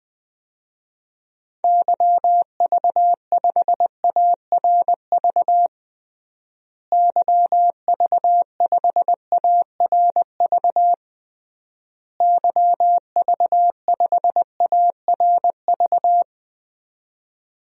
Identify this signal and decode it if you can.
Morse code, heard as a keyed tone, about 20 wpm, 700 Hz. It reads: YV5ARV YV5ARV YV5ARV
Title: DXCC-100 Call Signs - 6 Characters - Repeated 3x - 20wpm